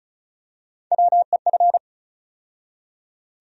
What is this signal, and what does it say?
Morse code, heard as a keyed tone, about 34 wpm, 700 Hz. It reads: WEF
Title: USA Call Sign Postfixes - 3 Letters 35wpm